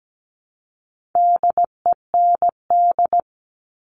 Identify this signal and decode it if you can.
Morse code, heard as a keyed tone, about 17 wpm, 700 Hz. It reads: DEND
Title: All Cut Numbers - 4-Digit 17wpm